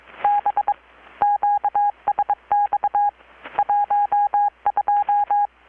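Morse code, heard as a keyed tone, about 22 wpm, 820 Hz. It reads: B QSX 12
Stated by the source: Morse from KPH's Night of Nights XX maritime radiotelegraphy event, received in Chiapas, Mexico about 3,800 km away on a Kenwood R-600 with a grounded 10 m long-wire antenna.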